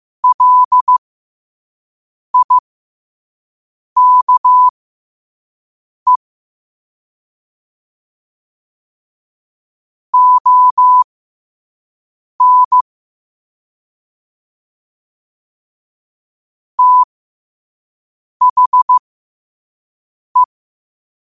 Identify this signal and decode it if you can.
Morse code, heard as a keyed tone, about 15 wpm, spaced as for 5 wpm, 1000 Hz. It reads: LIKE ON THE
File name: qotdfiles / 5wpm